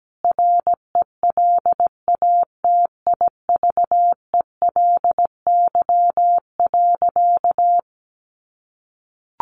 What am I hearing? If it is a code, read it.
Morse code, heard as a keyed tone, about 17 wpm, 700 Hz. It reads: RELATIVELY.
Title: Sentences from Top 1100 Words 17wpm